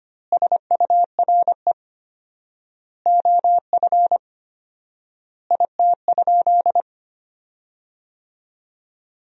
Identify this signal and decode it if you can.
Morse code, heard as a keyed tone, about 25 wpm, 700 Hz. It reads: SURE OF IT?
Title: 4x Word-Spacing - Sentences from Top 300 Words - 25wpm